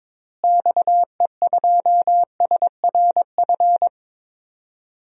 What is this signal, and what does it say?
Morse code, heard as a keyed tone, about 22 wpm, 700 Hz. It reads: XE2SRF
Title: DXCC-100 Call Signs - 6 Characters - Repeated 3x - 22wpm